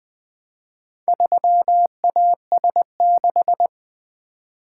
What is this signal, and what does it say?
Morse code, heard as a keyed tone, about 20 wpm, 700 Hz. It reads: 3AS6
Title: Four Letter-Number Combinations - 20wpm